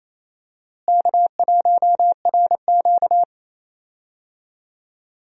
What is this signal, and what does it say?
Morse code, heard as a keyed tone, about 28 wpm, 700 Hz. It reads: K1RQ
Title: USA Call Signs - 4 Characters - Repeated 2x - 28wpm